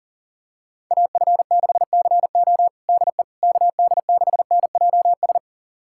Morse code, heard as a keyed tone, about 40 wpm, 700 Hz. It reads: AF6CO DE KD6NJS
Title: USA Callsign DE Callsign 40wpm